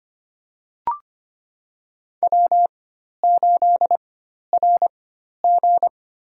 Morse code, heard as a keyed tone, about 25 wpm, spaced as for 12 wpm, 700 Hz. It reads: W8RG
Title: USA Call Signs - 4 Characters - 4x Character Spacing - 25wpm